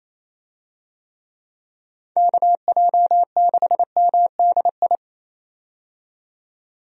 Morse code, heard as a keyed tone, about 28 wpm, 700 Hz. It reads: KJ6MDI
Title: USA Call Signs - 6 Characters - 28wpm